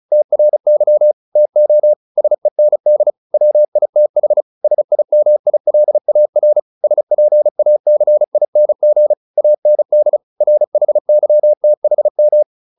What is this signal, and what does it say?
Morse code, heard as a keyed tone, about 34 wpm, 600 Hz. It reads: TRY TO SEND WITH SIMILAR SPACING AND RHYTHM